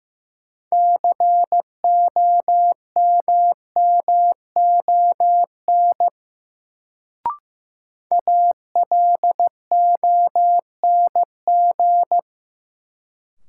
Morse code, heard as a keyed tone, about 15 wpm, 700 Hz. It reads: COMMON ALONG
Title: Sentences from Top 700 Words - New Word Drill - 15wpm